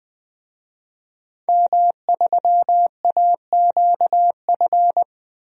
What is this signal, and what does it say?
Morse code, heard as a keyed tone, about 21 wpm, 700 Hz. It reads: M3AQF
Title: DXCC-100 Call Signs - 5 Characters - 20wpm